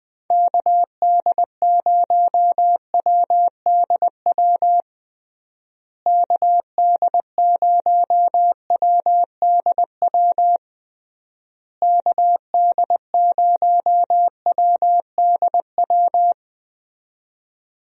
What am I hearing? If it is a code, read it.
Morse code, heard as a keyed tone, about 21 wpm, 700 Hz. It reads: KD0WDW KD0WDW KD0WDW
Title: USA Call Signs - 6 Characters - Repeated 3x - 20wpm